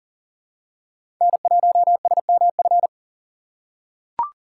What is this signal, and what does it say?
Morse code, heard as a keyed tone, about 40 wpm, 700 Hz. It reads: N1SMF
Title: USA Call Signs - 5 Characters - 2x Character Spacing - 40wpm